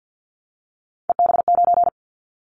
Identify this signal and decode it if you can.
Morse code, heard as a keyed tone, about 50 wpm, 700 Hz. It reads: EB9
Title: Callsign Prefixes for Top 100 DXCC - 50wpm